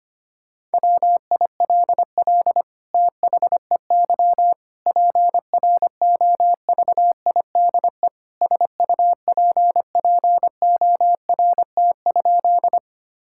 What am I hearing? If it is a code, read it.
Morse code, heard as a keyed tone, about 25 wpm, 700 Hz. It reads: WILL THEY PROVIDE SUPPORT?
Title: Sentences from Top 1000 Words - Review Part 5 - 25wpm